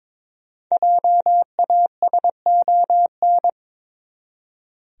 Morse code, heard as a keyed tone, about 21 wpm, 700 Hz. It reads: JASON
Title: USA Common Names - Max 5 Letters - 22wpm